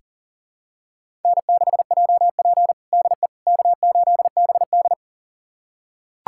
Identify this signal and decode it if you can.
Morse code, heard as a keyed tone, about 40 wpm, 700 Hz. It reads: N6JP DE K8BD
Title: USA Callsign DE Callsign - 4 Characters - 40wpm